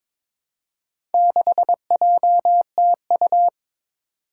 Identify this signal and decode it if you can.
Morse code, heard as a keyed tone, about 22 wpm, 700 Hz. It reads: 6JTU
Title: Four Challenging Letter-Number Combinations 22wpm